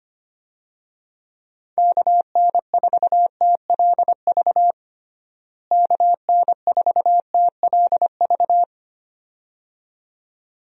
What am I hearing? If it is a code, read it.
Morse code, heard as a keyed tone, about 25 wpm, 700 Hz. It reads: KN4TLV KN4TLV
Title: USA Call Signs - 6 Characters - Repeated 2x - 25wpm